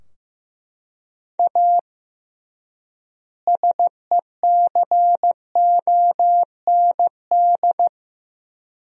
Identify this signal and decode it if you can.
Morse code, heard as a keyed tone, about 15 wpm, 700 Hz. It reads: A SECOND
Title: 3x Word-Spacing - Sentences from Top 300 Words - 15wpm